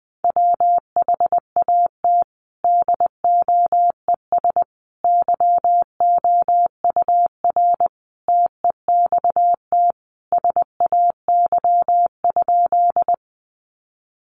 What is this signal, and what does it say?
Morse code, heard as a keyed tone, about 20 wpm, 700 Hz. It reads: WHAT DOES YOUR TEXT SAY?
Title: Sentences from Top 1200 Words 20wpm